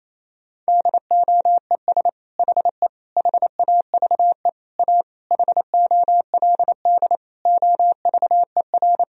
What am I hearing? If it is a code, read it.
Morse code, heard as a keyed tone, about 27 wpm, 700 Hz. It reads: DOES HE HAVE A HOLD OVER
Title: Sentences from Top 1000 Words - Review Part 2 - 28wpm